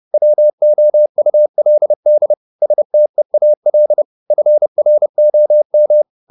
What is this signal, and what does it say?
Morse code, heard as a keyed tone, about 30 wpm, 600 Hz. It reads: WOULD STEAL FROM
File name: the old man and the sea / the old man and the sea 1 30wpm